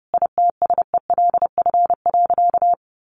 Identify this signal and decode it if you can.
Morse code, heard as a keyed tone, about 28 wpm, 700 Hz. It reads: ITSELF.
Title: Sentences from Top 1400 Words 30wpm